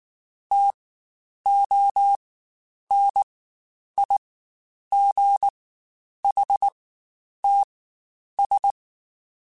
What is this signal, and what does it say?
Morse code, heard as a keyed tone, about 19 wpm, spaced as for 9 wpm, 800 Hz. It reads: TONIGHTS
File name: Spaced Characters / Spaced Characters-mp3s / Weather 3 17 sp @18